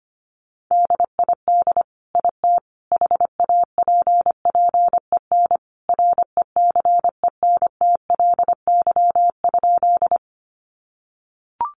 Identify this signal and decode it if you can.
Morse code, heard as a keyed tone, about 25 wpm, 700 Hz. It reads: DID IT HAPPEN RECENTLY?
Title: Sentences from Top 1200 Words 25wpm